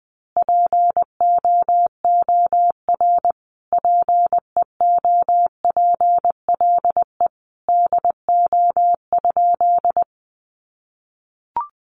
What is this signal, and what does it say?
Morse code, heard as a keyed tone, about 20 wpm, 700 Hz. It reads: POOR PEOPLE DO?
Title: Sentences from Top 1100 Words 20wpm